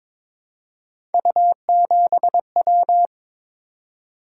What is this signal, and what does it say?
Morse code, heard as a keyed tone, about 22 wpm, 700 Hz. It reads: U7W
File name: practice / 22wpm / Three Letter-Number Combinations - 22wpm